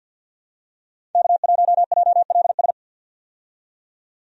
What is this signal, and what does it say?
Morse code, heard as a keyed tone, about 50 wpm, 700 Hz. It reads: K1JLS
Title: USA Call Signs - 5 Characters - 50wpm